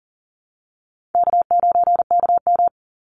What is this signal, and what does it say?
Morse code, heard as a keyed tone, about 39 wpm, 700 Hz. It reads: K9KM
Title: Easy Call Signs Repeated 3x - 40wpm